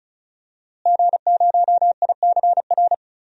Morse code, heard as a keyed tone, about 34 wpm, 690 Hz. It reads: G0ICR
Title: DXCC-100 Call Signs - 5 Characters - Repeated 3x - 35wpm